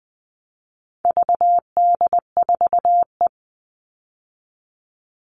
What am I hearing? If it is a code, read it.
Morse code, heard as a keyed tone, about 20 wpm, 700 Hz. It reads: VD4E
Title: All Cut Numbers - 4-Digit 20wpm